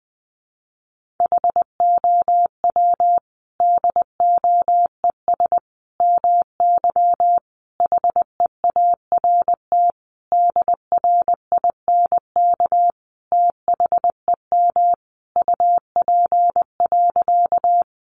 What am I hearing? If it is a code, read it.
Morse code, heard as a keyed tone, about 20 wpm, 700 Hz. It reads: HOW DOES MY HEART DRINK THEM UP.
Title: Sentences from Top 1500 Words 20wpm